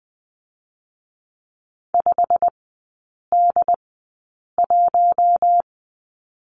Morse code, heard as a keyed tone, about 21 wpm, spaced as for 8 wpm, 700 Hz. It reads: 5D1
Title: 087 Lesson 15e - Number 5 - Three Characters 20wpm